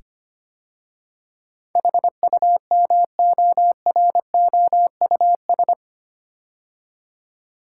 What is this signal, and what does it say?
Morse code, heard as a keyed tone, about 25 wpm, 700 Hz. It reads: HUMOROUS
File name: 8 Letter Words - First 3 Letters Given - 25wpm